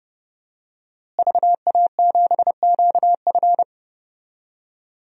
Morse code, heard as a keyed tone, about 30 wpm, 700 Hz. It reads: VA7QF
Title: DXCC-100 Call Signs - 5 Characters - Repeated 3x - 30wpm